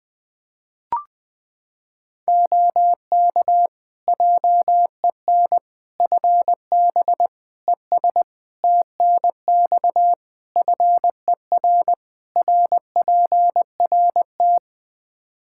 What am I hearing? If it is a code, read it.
Morse code, heard as a keyed tone, about 21 wpm, 700 Hz. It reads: OK JEN FB ES TNX FER RPRT
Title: Practice QSO - Exchange 2 - Info - 20wpm